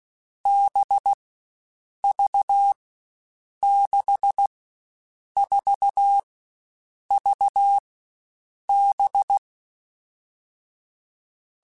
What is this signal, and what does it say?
Morse code, heard as a keyed tone, about 16 wpm, 800 Hz. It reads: B V 6 4 V B
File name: Groups@13 / Groups@13 mp3s / VB46@13